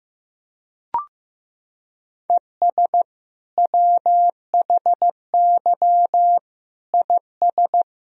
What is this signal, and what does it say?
Morse code, heard as a keyed tone, about 15 wpm, 700 Hz. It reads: ES WHY IS
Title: Sentences from Top 200 Words and Abbreviations 15wpm